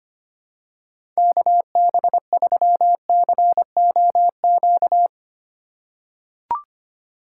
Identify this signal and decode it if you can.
Morse code, heard as a keyed tone, about 25 wpm, 700 Hz. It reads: KB3COQ
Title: USA Call Signs - 6 Characters - 8x Character Spacing - 25wpm